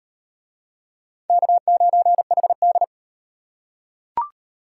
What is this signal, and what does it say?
Morse code, heard as a keyed tone, about 38 wpm, 700 Hz. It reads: K9HD